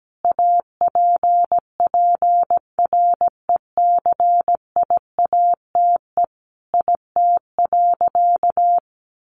Morse code, heard as a keyed tone, about 17 wpm, 700 Hz. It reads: APPRECIATE IT.
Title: Sentences from Top 1300 Words 17wpm